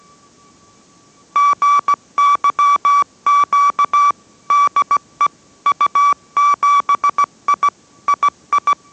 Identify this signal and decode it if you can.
Morse code, heard as a keyed tone, about 17 wpm, 1150 Hz. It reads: GYQ DE U7I II